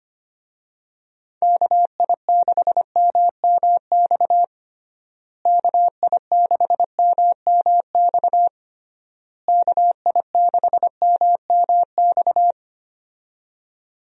Morse code, heard as a keyed tone, about 25 wpm, 700 Hz. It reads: KI6MMX KI6MMX KI6MMX
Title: USA Call Signs - 6 Characters - Repeated 3x - 25wpm